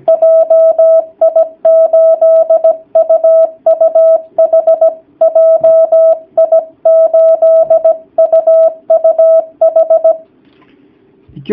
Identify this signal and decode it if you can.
Morse code, heard as a keyed tone, about 17 wpm, 650 Hz. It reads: JI8UUH JI8UUH
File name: jh8cbh / 2019cw / sp16- w